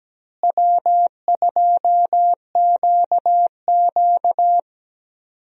Morse code, heard as a keyed tone, about 17 wpm, 700 Hz. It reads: W2QQ
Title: USA Call Signs - 4 Characters - Repeated 3x - 17wpm